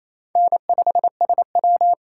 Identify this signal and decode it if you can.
Morse code, heard as a keyed tone, about 34 wpm, 700 Hz. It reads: N5SW
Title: Speed Racing - USA 1x2 Call Signs 28wpm